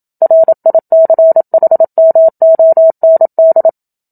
Morse code, heard as a keyed tone, about 27 wpm, 650 Hz. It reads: RICHMOND
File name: Book Two / 01-Chapter 1 - 27 WPM